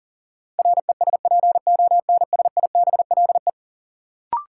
Morse code, heard as a keyed tone, about 40 wpm, 700 Hz. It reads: RESPONSIBLE
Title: Sentences from Top 1000 Words - Review Part 7 - New Word Drill - 40wpm